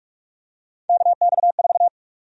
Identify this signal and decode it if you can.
Morse code, heard as a keyed tone, about 46 wpm, 700 Hz. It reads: KX4
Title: USA Call Sign Prefixes - 45wpm